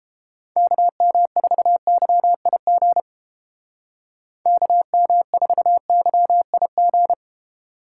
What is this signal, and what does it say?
Morse code, heard as a keyed tone, about 31 wpm, 700 Hz. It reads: KM4YIG KM4YIG